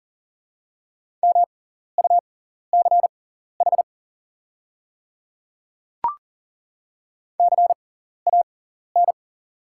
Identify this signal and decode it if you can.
Morse code, heard as a keyed tone, about 40 wpm, spaced as for 14 wpm, 700 Hz. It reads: MUCH CAN
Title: 6x Character Spacing - Top 300 Words - 40wpm